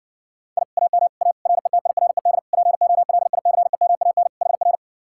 Morse code, heard as a keyed tone, about 69 wpm, 700 Hz. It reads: I AM A RETIRED POLICEMAN HR